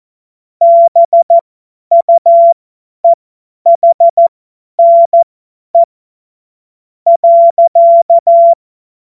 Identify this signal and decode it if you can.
Morse code, heard as a keyed tone, about 14 wpm, spaced as for 10 wpm, 680 Hz. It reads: BUEHNE .
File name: Morsekurs / Witze / Witz 04. Oktober 2022 50 F70